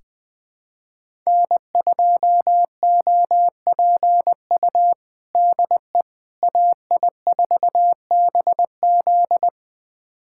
Morse code, heard as a keyed tone, about 20 wpm, 700 Hz. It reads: N2OPU DE AI4BZ
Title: USA Callsign DE Callsign 20wpm